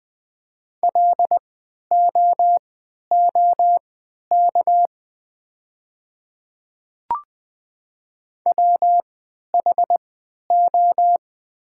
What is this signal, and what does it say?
Morse code, heard as a keyed tone, about 20 wpm, spaced as for 11 wpm, 700 Hz. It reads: LOOK WHO
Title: 3x Character Spacing - Top 100 Words - 20wpm